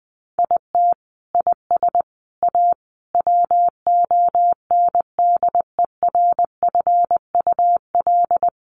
Morse code, heard as a keyed tone, about 20 wpm, 700 Hz. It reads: IT IS A WONDERFUL
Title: Sentences from Top 1500 Words 20wpm